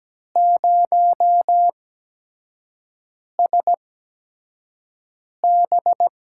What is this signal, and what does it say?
Morse code, heard as a keyed tone, about 17 wpm, spaced as for 5 wpm, 700 Hz. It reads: 0SB
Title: USA Call Signs - 6 Characters - 8x Character Spacing - 17wpm